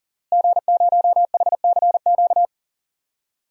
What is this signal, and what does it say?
Morse code, heard as a keyed tone, about 40 wpm, 700 Hz. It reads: G0HCQ
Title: DXCC-100 Call Signs - 5 Characters - Repeated 3x - 40wpm